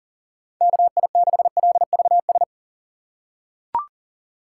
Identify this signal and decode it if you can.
Morse code, heard as a keyed tone, about 37 wpm, 700 Hz. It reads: KI6LVS
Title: USA Call Signs - 3x Character Spacing - 40wpm